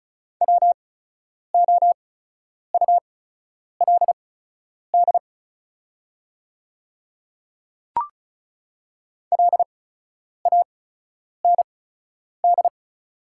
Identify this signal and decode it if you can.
Morse code, heard as a keyed tone, about 36 wpm, spaced as for 10 wpm, 700 Hz. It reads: WOULD LAND
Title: 8x Character Spacing - Top 500 Words - 35wpm